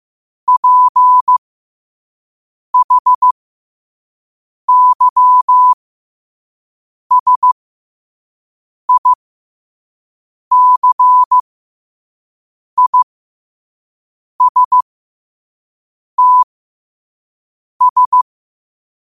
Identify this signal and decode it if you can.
Morse code, heard as a keyed tone, about 15 wpm, spaced as for 5 wpm, 1000 Hz. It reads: PHYSICISTS
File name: qotdfiles / 5wpm